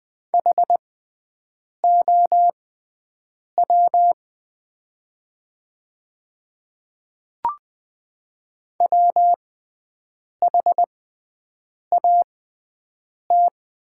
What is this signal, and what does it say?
Morse code, heard as a keyed tone, about 20 wpm, spaced as for 7 wpm, 700 Hz. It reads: HOW WHAT
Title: 6x Character Spacing - Top 100 Words - 20wpm